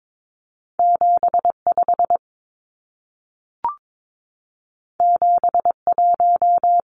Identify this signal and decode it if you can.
Morse code, heard as a keyed tone, about 22 wpm, 700 Hz. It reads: 75 71
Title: Numbers - 2-Digit 22wpm